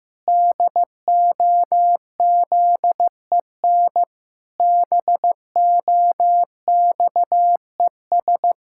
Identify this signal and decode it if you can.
Morse code, heard as a keyed tone, about 15 wpm, 700 Hz. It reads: DOZEN BOXES
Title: Pangram Challenge v2 - 15wpm